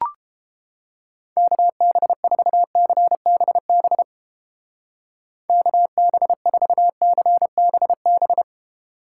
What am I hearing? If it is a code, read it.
Morse code, heard as a keyed tone, about 29 wpm, 700 Hz. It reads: KB4CBB KB4CBB